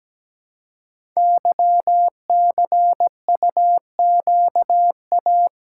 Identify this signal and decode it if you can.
Morse code, heard as a keyed tone, about 17 wpm, 700 Hz. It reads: YCUQA
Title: Code Groups - 5 Letters - 17wpm